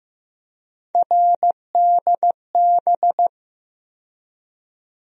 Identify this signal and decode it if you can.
Morse code, heard as a keyed tone, about 15 wpm, 700 Hz. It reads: RDB
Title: USA Call Sign Postfixes - 3 Letters 15wpm